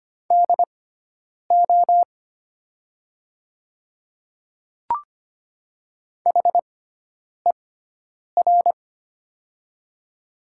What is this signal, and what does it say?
Morse code, heard as a keyed tone, about 25 wpm, spaced as for 9 wpm, 700 Hz. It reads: DO HER
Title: 6x Character Spacing - Top 100 Words - 25wpm